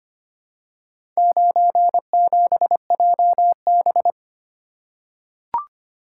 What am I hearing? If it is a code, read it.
Morse code, heard as a keyed tone, about 25 wpm, 700 Hz. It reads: 97JB
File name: practice / 25wpm / Four Letter-Number Combinations - 25wpm